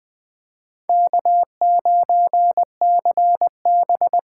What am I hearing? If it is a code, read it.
Morse code, heard as a keyed tone, about 20 wpm, 700 Hz. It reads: K9CB